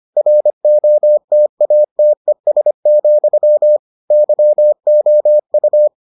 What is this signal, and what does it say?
Morse code, heard as a keyed tone, about 25 wpm, 600 Hz. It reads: ROTATES, YOU